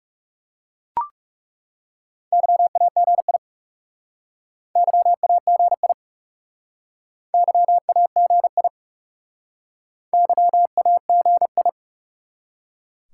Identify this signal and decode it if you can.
Morse code, heard as a keyed tone, about 39 wpm, 700 Hz. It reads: YAGI YAGI YAGI YAGI